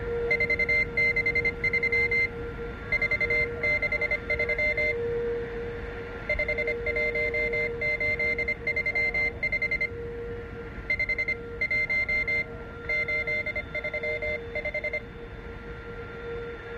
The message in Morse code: 463 463 51835 51835